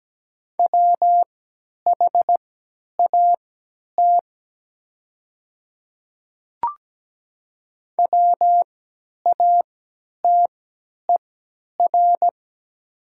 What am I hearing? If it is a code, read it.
Morse code, heard as a keyed tone, about 17 wpm, spaced as for 10 wpm, 700 Hz. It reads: WHAT WATER
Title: 3x Character Spacing - Top 200 Words - 17wpm